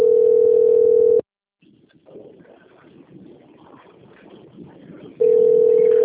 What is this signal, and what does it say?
Ringback tone (North America), 440 + 480 Hz.
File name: out- - - - - .30433